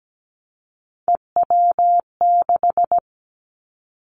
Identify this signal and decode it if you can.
Morse code, heard as a keyed tone, about 17 wpm, 700 Hz. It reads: EW6
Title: Callsign Prefixes for Top 50 DXCC - 17wpm